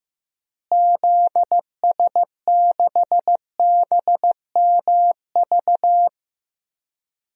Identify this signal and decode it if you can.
Morse code, heard as a keyed tone, about 15 wpm, 700 Hz. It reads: ZS6BMV
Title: DXCC-100 Call Signs - 6 Characters - Repeated 3x - 15wpm